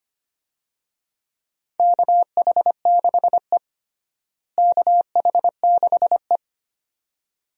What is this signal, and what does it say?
Morse code, heard as a keyed tone, about 25 wpm, 700 Hz. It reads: KH6E KH6E
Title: DXCC-100 Call Signs - 4 Characters - Repeated 2x - 25wpm